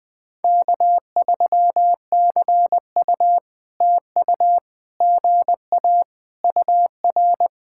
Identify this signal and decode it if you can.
Morse code, heard as a keyed tone, about 21 wpm, 700 Hz. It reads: K3CU TU GA UR
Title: POTA QSO Practice - 20wpm